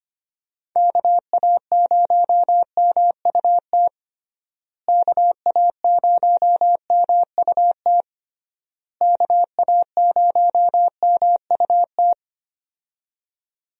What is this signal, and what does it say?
Morse code, heard as a keyed tone, about 25 wpm, 700 Hz. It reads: KA0MUT KA0MUT KA0MUT
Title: USA Call Signs - 6 Characters - Repeated 3x - 25wpm